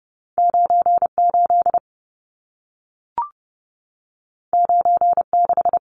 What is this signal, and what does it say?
Morse code, heard as a keyed tone, about 30 wpm, 700 Hz. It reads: 98 96
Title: Numbers - 2-Digit 30wpm